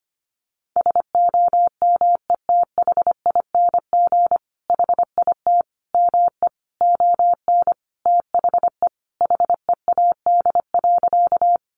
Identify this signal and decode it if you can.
Morse code, heard as a keyed tone, about 25 wpm, 700 Hz. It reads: SOMETHING HIT ME ON THE HEAD.
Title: Sentences from Top 1200 Words 25wpm